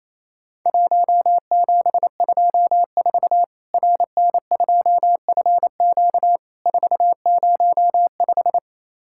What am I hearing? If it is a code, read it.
Morse code, heard as a keyed tone, about 28 wpm, 700 Hz. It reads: 1724 RN2FQ 405
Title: Worked All Europe DX Contest - QTC Exchange - 28wpm